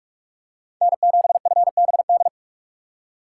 Morse code, heard as a keyed tone, about 45 wpm, 700 Hz. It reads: N7FBD
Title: USA Call Signs - 5 Characters - 6x Character Spacing - 45wpm